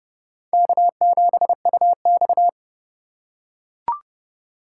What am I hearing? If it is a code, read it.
Morse code, heard as a keyed tone, about 29 wpm, 700 Hz. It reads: K7UX